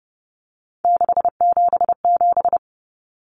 Morse code, heard as a keyed tone, about 30 wpm, 700 Hz. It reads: 677
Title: Numbers - 3-Digit 30wpm